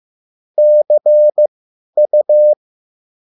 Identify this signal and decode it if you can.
Morse code, heard as a keyed tone, about 15 wpm, spaced as for 10 wpm, 600 Hz. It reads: CU